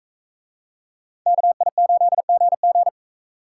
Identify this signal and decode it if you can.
Morse code, heard as a keyed tone, about 38 wpm, 700 Hz. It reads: KI8GG